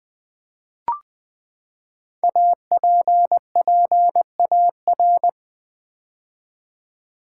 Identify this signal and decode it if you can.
Morse code, heard as a keyed tone, about 20 wpm, 700 Hz. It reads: APPAR
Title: Build A Word - 5 Letters At A Time - 10-Letter Words - 20wpm